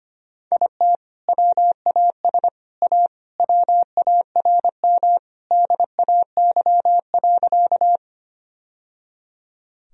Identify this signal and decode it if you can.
Morse code, heard as a keyed tone, about 25 wpm, 700 Hz. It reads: IT WAS A WARM DAY.
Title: Sentences from Top 1000 Words - Review Part 2 - 25wpm